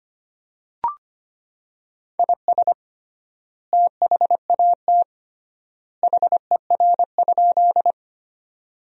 Morse code, heard as a keyed tone, about 25 wpm, 700 Hz. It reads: IS THAT HER?
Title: 3x Word-Spacing - Sentences from Top 100 Words - 25wpm